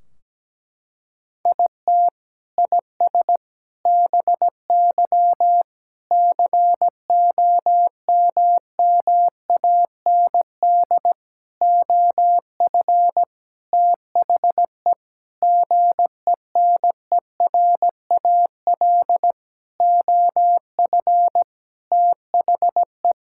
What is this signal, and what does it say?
Morse code, heard as a keyed tone, about 17 wpm, 700 Hz. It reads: IT IS BY COMMAND OF THE GENERAL OF THE
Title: Sentences from Top 1600 Words 17wpm